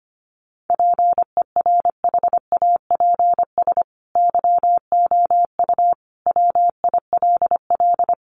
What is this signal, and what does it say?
Morse code, heard as a keyed tone, about 25 wpm, 700 Hz. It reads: PERHAPS YOU WILL
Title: Sentences from Top 1400 Words 25wpm